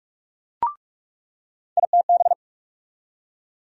Morse code, heard as a keyed tone, about 45 wpm, 700 Hz. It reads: ITB
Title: USA Call Sign Postfixes - 3 Letters 45wpm